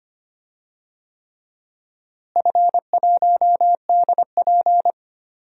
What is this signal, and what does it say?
Morse code, heard as a keyed tone, about 25 wpm, 700 Hz. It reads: F1DP
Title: DXCC-100 Call Signs - 4 Characters - Repeated 3x - 25wpm